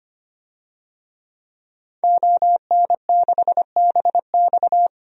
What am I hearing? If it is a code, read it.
Morse code, heard as a keyed tone, about 25 wpm, 700 Hz. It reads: ON6BX
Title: DXCC-100 Call Signs - 5 Characters - 25wpm